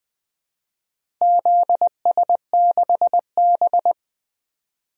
Morse code, heard as a keyed tone, about 20 wpm, 700 Hz. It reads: ZS6B
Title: DXCC-100 Call Signs - 4 Characters - Repeated 3x - 20wpm